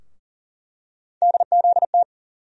Morse code, heard as a keyed tone, about 40 wpm, 700 Hz. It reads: DZT
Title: USA Call Sign Postfixes - 3 Letters 40wpm